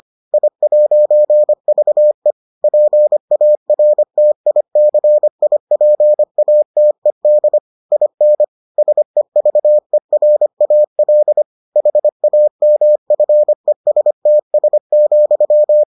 Morse code, heard as a keyed tone, about 25 wpm, 600 Hz. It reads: I'VE PARTICIPATED IN SEVERAL HAMFESTS,